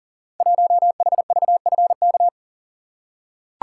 Morse code, heard as a keyed tone, about 39 wpm, 700 Hz. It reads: 1HVFK
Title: Code Groups - 5 Letter:Numbers - 40wpm